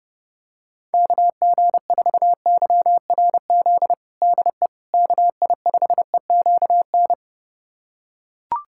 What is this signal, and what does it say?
Morse code, heard as a keyed tone, about 30 wpm, 700 Hz. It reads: KG4YRZ DE KI5EQN
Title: USA Callsign DE Callsign - 6 Characters - 30wpm